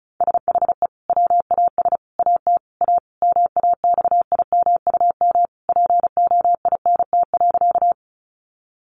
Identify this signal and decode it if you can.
Morse code, heard as a keyed tone, about 34 wpm, 700 Hz. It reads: SHE WAS AT A MAXIMUM POINT.
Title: Sentences from Top 1300 Words 35wpm